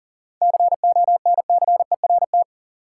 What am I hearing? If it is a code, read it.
Morse code, heard as a keyed tone, about 40 wpm, 700 Hz. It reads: CONCERT
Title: Seven Letter Words - 40wpm